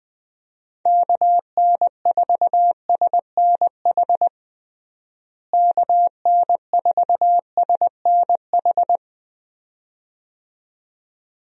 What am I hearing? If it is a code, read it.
Morse code, heard as a keyed tone, about 21 wpm, 700 Hz. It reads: KN4SNH KN4SNH
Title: USA Call Signs - 6 Characters - Repeated 2x - 20wpm